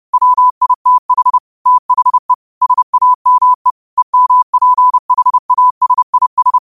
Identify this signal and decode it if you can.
Morse code, heard as a keyed tone, about 30 wpm, 1000 Hz. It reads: WITH THE SAME EMPHASIS